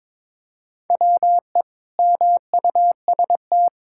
Morse code, heard as a keyed tone, about 22 wpm, 700 Hz. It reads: WE MUST
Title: Sentences from Top 1000 Words - Review Part 5 - 22wpm